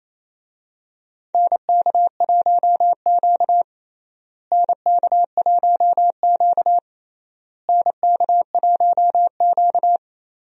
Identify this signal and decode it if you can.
Morse code, heard as a keyed tone, about 27 wpm, 700 Hz. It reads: NK1Q NK1Q NK1Q
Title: USA Call Signs - 4 Characters - Repeated 3x - 28wpm